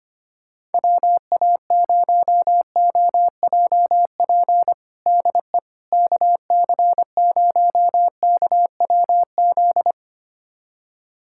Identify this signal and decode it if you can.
Morse code, heard as a keyed tone, about 25 wpm, 700 Hz. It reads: WA0OJP DE KC0KWZ
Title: USA Callsign DE Callsign - 6 Characters - 25wpm